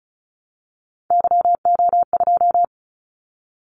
Morse code, heard as a keyed tone, about 36 wpm, 700 Hz. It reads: YO2
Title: Callsign Prefixes for Top 25 DXCC - 35wpm